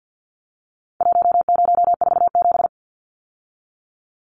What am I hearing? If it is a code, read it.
Morse code, heard as a keyed tone, about 50 wpm, 700 Hz. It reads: 1047
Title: Numbers - Contest Style - 4-Digit 50wpm